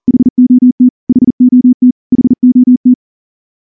Morse code, heard as a keyed tone, about 38 wpm, 250 Hz. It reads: HOT HOT HOT
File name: morse-training-40WPM